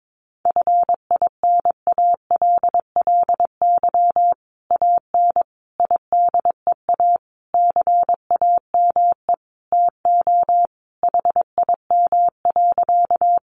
Morse code, heard as a keyed tone, about 22 wpm, 700 Hz. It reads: FINALLY AN IDEA CAME TO HIM.
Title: Sentences from Top 1000 Words 22wpm